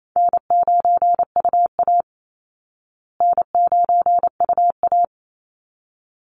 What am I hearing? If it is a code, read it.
Morse code, heard as a keyed tone, about 28 wpm, 700 Hz. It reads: N9UA N9UA
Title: Call Signs Repeated 3x - 28wpm